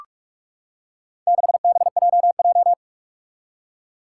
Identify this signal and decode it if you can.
Morse code, heard as a keyed tone, about 45 wpm, 700 Hz. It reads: 6BJJ
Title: Four Challenging Letter-Number Combinations 45wpm